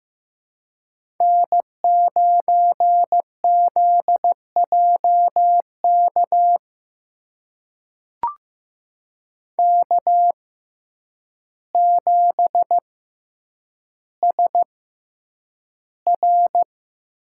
Morse code, heard as a keyed tone, about 15 wpm, spaced as for 8 wpm, 700 Hz. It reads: N9ZJK K7SR
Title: USA Call Signs - 5 Characters - 6x Character Spacing - 15wpm